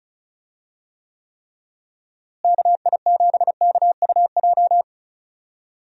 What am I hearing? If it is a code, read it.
Morse code, heard as a keyed tone, about 36 wpm, 700 Hz. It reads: KI7KUJ
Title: USA Call Signs - 6 Characters - Repeated 3x - 35wpm